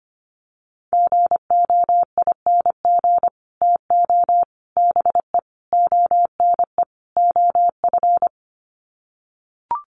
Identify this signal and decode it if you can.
Morse code, heard as a keyed tone, about 25 wpm, 700 Hz. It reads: GOING TO BE ONE OF
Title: Sets of 5 Words Encore2 25wpm